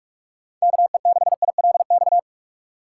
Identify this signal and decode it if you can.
Morse code, heard as a keyed tone, about 46 wpm, 690 Hz. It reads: KE6ILX
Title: USA Call Signs - 6 Characters - 2x Character Spacing - 45wpm